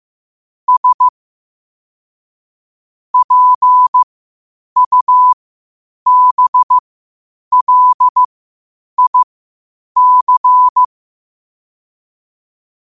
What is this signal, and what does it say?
Morse code, heard as a keyed tone, about 15 wpm, spaced as for 8 wpm, 1000 Hz. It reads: S PUBLIC
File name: qotdfiles / 8wpm